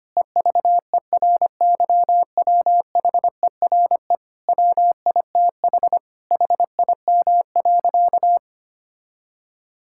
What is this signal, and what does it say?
Morse code, heard as a keyed tone, about 25 wpm, 700 Hz. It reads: EVERYWHERE WITH HIM.
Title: Sentences from Top 1600 Words 25wpm